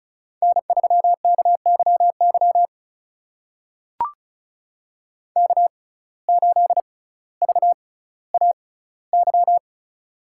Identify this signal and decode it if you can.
Morse code, heard as a keyed tone, about 36 wpm, spaced as for 18 wpm, 700 Hz. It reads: N3KYY K8VAY
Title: USA Call Signs - 6x Character Spacing - 35wpm